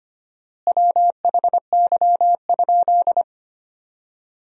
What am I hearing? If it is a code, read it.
Morse code, heard as a keyed tone, about 25 wpm, 700 Hz. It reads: WHY?
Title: 4x Word-Spacing - Sentences from Top 200 Words - 25wpm